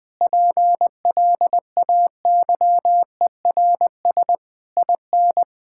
Morse code, heard as a keyed tone, about 20 wpm, 700 Hz. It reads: PLAYERS IN
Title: Sentences with 7-Word Phrases - Spoken Beginning - 20wpm